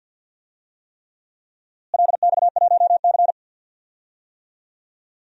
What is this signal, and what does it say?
Morse code, heard as a keyed tone, about 52 wpm, 700 Hz. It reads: LX1C
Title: DXCC-100 Call Signs - 4 Characters - Repeated 3x - 50wpm